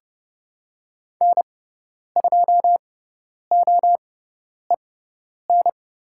Morse code, heard as a keyed tone, about 30 wpm, spaced as for 10 wpm, 700 Hz. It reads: N2OEN
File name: Lesson 16g - Letter c - Call Signs 30wpm